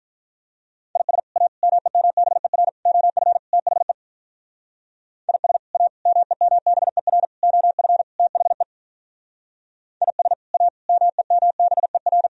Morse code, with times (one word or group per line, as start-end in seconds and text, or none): 1.0-1.2 IS
1.4-1.5 A
1.6-2.7 MEMBER
2.9-3.4 OF
3.5-3.9 THE
5.3-5.6 IS
5.7-5.9 A
6.1-7.2 MEMBER
7.4-8.0 OF
8.2-8.6 THE
10.0-10.3 IS
10.5-10.7 A
10.9-12.3 MEMBER